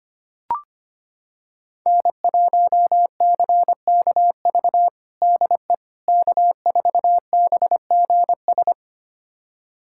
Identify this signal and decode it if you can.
Morse code, heard as a keyed tone, about 25 wpm, 700 Hz. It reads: N1CKV DE K4BGS
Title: USA Callsign DE Callsign - 5 Characters - 25wpm